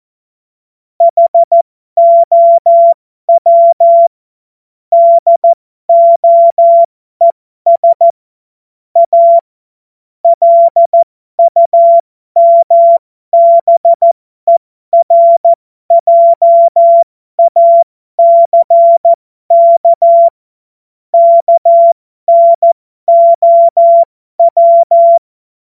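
Morse code, heard as a keyed tone, about 14 wpm, spaced as for 12 wpm, 680 Hz. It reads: HOW DOES A LUMBERJACK KNOW
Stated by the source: Amateur radio CW practice text made of jokes.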